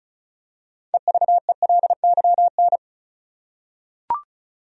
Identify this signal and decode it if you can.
Morse code, heard as a keyed tone, about 36 wpm, 700 Hz. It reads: EVELYN
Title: USA Common Names - 35wpm